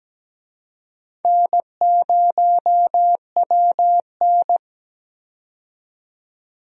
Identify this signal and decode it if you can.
Morse code, heard as a keyed tone, about 17 wpm, 700 Hz. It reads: N0WN
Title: USA Call Signs - 4 Characters - 17wpm